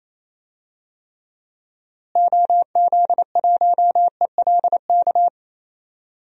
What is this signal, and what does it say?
Morse code, heard as a keyed tone, about 28 wpm, 700 Hz. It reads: OZ1ELK
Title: DXCC-100 Call Signs - 6 Characters - Repeated 2x - 28wpm